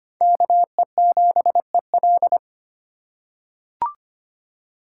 Morse code, heard as a keyed tone, about 25 wpm, 700 Hz. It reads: KE7EL
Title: USA Call Signs - 5 Characters - 8x Character Spacing - 25wpm